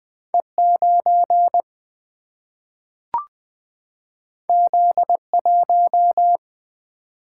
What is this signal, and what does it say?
Morse code, heard as a keyed tone, about 20 wpm, 700 Hz. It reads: E9 Z1
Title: Single-Letter Single-Number 20wpm